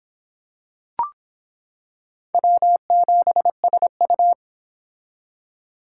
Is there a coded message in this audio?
Yes: Morse code from 2.3 to 4.3 s, heard as a keyed tone, about 25 wpm, 700 Hz. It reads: W7SU